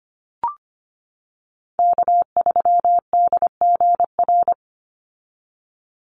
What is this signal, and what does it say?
Morse code, heard as a keyed tone, about 25 wpm, 700 Hz. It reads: K3DGR
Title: Call Signs - 25wpm